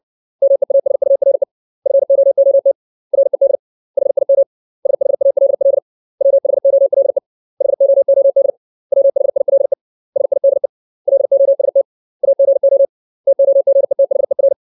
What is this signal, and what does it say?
Morse code, heard as a keyed tone, about 60 wpm, 550 Hz. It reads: MEASURE FOOT RED HIM SHALL WHOLE HOOD WHILE SIDE LOST AGO TOGETHER
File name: 632 words / 632 Words 60WPM